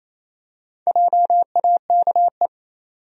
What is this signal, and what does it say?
Morse code, heard as a keyed tone, about 28 wpm, 700 Hz. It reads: JAKE
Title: USA Common Names - Max 5 Letters - 28wpm